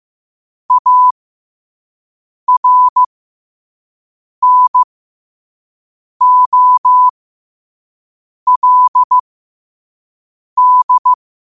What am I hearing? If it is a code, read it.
Morse code, heard as a keyed tone, about 15 wpm, spaced as for 5 wpm, 1000 Hz. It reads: ARNOLD
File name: qotdfiles / 5wpm